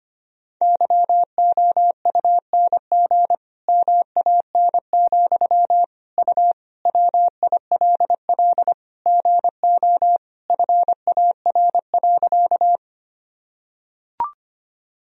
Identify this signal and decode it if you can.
Morse code, heard as a keyed tone, about 25 wpm, 700 Hz. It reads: YOUNG MAN, U WILL GO FAR.
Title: Sentences from Top 400 Words and Abbreviations 25wpm